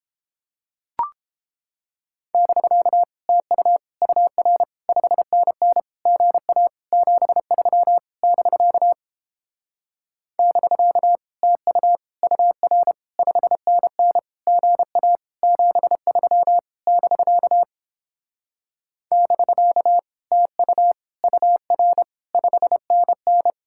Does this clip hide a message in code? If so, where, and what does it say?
Yes: Morse code from 2.3 to 23.5 s, heard as a keyed tone, about 31 wpm, 700 Hz. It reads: <BK> TU UR 5NN GA 73 <BK> <BK> TU UR 5NN GA 73 <BK> <BK> TU UR 5NN